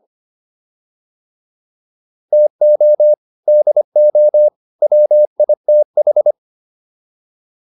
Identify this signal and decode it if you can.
Morse code, heard as a keyed tone, about 25 wpm, 600 Hz. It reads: TO DO WITH
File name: WD205 25